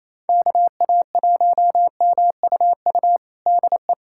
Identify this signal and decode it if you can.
Morse code, heard as a keyed tone, about 28 wpm, 700 Hz. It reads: KA1MUU DE
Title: USA Callsign DE Callsign - 6 Characters - 28wpm